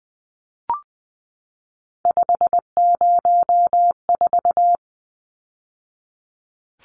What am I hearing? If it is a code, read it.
Morse code, heard as a keyed tone, about 20 wpm, 700 Hz. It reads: 504
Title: Numbers - 3-Digit 20wpm